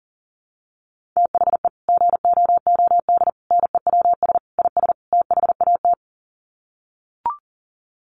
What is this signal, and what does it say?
Morse code, heard as a keyed tone, about 40 wpm, 700 Hz. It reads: THE GOOD NEWS IS THAT
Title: Sets of 5 Words 40wpm